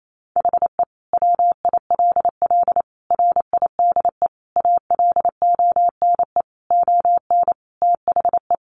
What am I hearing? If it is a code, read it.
Morse code, heard as a keyed tone, about 28 wpm, 700 Hz. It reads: HE WILL RIDE ALONE ON THE
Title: Sentences from Top 1400 Words 28wpm